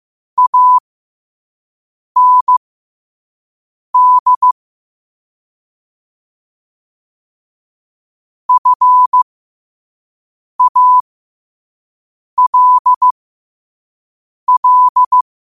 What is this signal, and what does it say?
Morse code, heard as a keyed tone, about 15 wpm, spaced as for 5 wpm, 1000 Hz. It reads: AND FALL
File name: qotdfiles / 5wpm